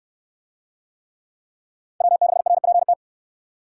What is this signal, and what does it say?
Morse code, heard as a keyed tone, about 64 wpm, 700 Hz. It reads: W6RPT